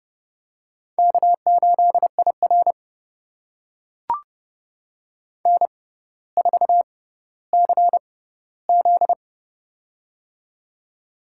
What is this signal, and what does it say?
Morse code, heard as a keyed tone, about 30 wpm, spaced as for 15 wpm, 700 Hz. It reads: K8IR N4CZ
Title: USA Call Signs - 4 Characters - 6x Character Spacing - 30wpm